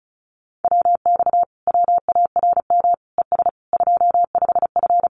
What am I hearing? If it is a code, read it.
Morse code, heard as a keyed tone, about 34 wpm, 700 Hz. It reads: WX WARM ES 25F
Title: Common QSO Phrases Encore 35wpm